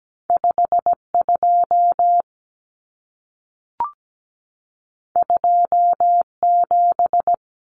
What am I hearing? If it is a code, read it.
Morse code, heard as a keyed tone, about 17 wpm, 700 Hz. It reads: 52 27
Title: Numbers - 2-Digit 17wpm